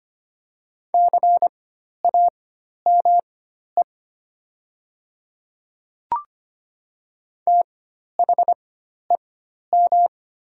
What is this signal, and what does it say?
Morse code, heard as a keyed tone, about 25 wpm, spaced as for 12 wpm, 700 Hz. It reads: CAME THEM
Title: 4x Character Spacing - Top 200 Words - 25wpm